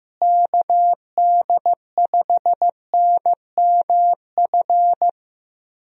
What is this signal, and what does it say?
Morse code, heard as a keyed tone, about 15 wpm, 700 Hz. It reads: KD5NMF
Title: USA Call Signs - 6 Characters - Repeated 3x - 15wpm